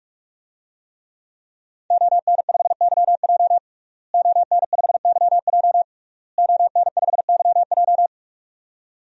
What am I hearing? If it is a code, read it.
Morse code, heard as a keyed tone, about 41 wpm, 700 Hz. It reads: ON5YJ ON5YJ ON5YJ
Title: DXCC-100 Call Signs - 5 Characters - Repeated 3x - 45wpm